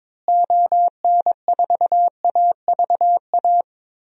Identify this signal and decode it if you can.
Morse code, heard as a keyed tone, about 22 wpm, 700 Hz. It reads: ON4AVA
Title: DXCC-100 Call Signs - 6 Characters - Repeated 3x - 22wpm